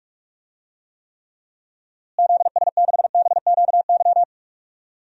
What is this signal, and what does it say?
Morse code, heard as a keyed tone, about 46 wpm, 700 Hz. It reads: ZS6BQY
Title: DXCC-100 Call Signs - 6 Characters - Repeated 3x - 45wpm